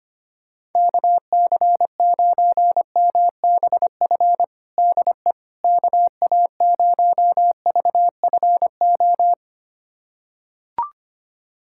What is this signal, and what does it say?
Morse code, heard as a keyed tone, about 25 wpm, 700 Hz. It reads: KC9MBF DE KA0VFO
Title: USA Callsign DE Callsign - 6 Characters - 25wpm